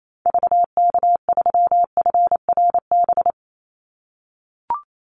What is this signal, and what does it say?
Morse code, heard as a keyed tone, about 28 wpm, 700 Hz. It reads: VK3FRB
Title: USA Call Signs 28wpm